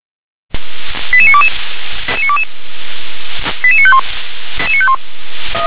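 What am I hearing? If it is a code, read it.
Five-tone selective call (ZVEI), heard as quick runs of five tones, tones of about 70 ms. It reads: 88922 88951
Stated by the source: ZVEI five-tone selective call received on SDR.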